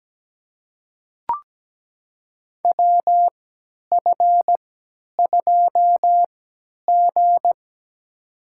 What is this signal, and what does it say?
Morse code, heard as a keyed tone, about 17 wpm, spaced as for 10 wpm, 700 Hz. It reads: WF2G
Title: USA Call Signs - 4 Characters - 3x Character Spacing - 17wpm